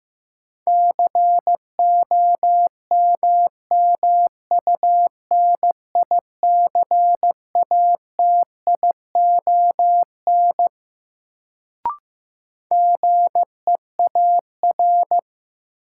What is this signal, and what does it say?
Morse code, heard as a keyed tone, about 15 wpm, 700 Hz. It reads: COMMUNICATION GEAR
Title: Sentences from Top 1400 Words - New Word Drill Alt - 15wpm